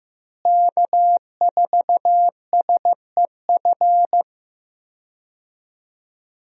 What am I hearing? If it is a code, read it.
Morse code, heard as a keyed tone, about 15 wpm, 700 Hz. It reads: K4SEF
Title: USA Call Signs - 5 Characters - Repeated 2x - 15wpm